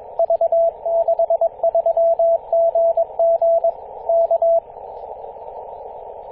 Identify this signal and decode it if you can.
Morse code, heard as a keyed tone, about 20 wpm, 670 Hz, with radio noise. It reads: V63GG K
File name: posts / 2016-apr-24-1040z-v63gg-14-016mhz-cw